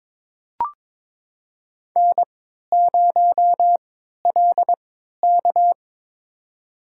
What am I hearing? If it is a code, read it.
Morse code, heard as a keyed tone, about 22 wpm, spaced as for 13 wpm, 700 Hz. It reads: N0LK
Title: USA Call Signs - 4 Characters - 3x Character Spacing - 22wpm